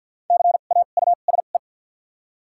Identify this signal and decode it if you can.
Morse code, heard as a keyed tone, about 53 wpm, spaced as for 37 wpm, 700 Hz. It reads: CAUSE
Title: 2x Character Spacing - Top 300 Words - 50wpm